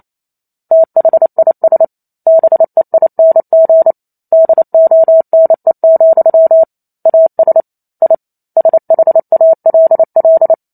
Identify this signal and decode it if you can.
Morse code, heard as a keyed tone, about 28 wpm, 650 Hz. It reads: THIS BEING DONE, AS I SHALL